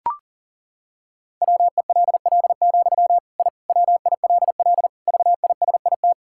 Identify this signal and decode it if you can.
Morse code, heard as a keyed tone, about 40 wpm, 700 Hz. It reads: WELL, I WILL VISIT
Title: Sentences from Top 1000 Words - Review Part 6 - 40wpm